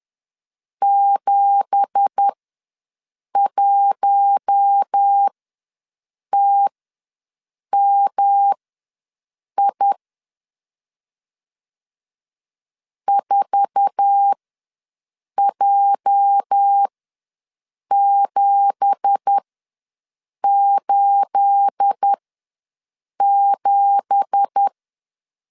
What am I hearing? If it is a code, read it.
Morse code, heard as a keyed tone, about 10 wpm, spaced as for 6 wpm, 800 Hz. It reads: 71TMI 4J787
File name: LEZIONI / 9 2867 ws 10wpm 4215